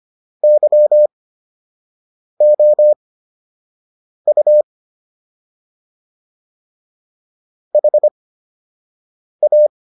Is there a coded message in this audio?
Yes: Morse code from 0.4 to 9.7 s, heard as a keyed tone, about 25 wpm, spaced as for 6 wpm, 600 Hz. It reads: YOU HA